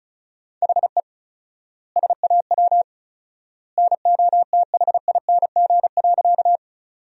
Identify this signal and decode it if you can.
Morse code, heard as a keyed tone, about 34 wpm, 700 Hz. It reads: HE SAW NOTHING.
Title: 4x Word-Spacing - Sentences from Top 500 Words - 35wpm